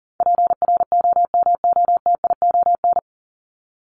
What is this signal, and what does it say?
Morse code, heard as a keyed tone, about 40 wpm, 700 Hz. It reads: PROMOTION
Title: Top 1501-1750 Words 40wpm